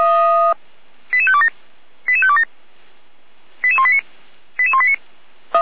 Five-tone selective call (ZVEI), heard as quick runs of five tones, tones of about 70 ms. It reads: 88527 88527 88189 88189